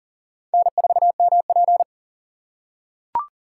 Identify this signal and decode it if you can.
Morse code, heard as a keyed tone, about 40 wpm, 700 Hz. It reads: N4MP
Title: USA Call Signs - 4x Character Spacing - 40wpm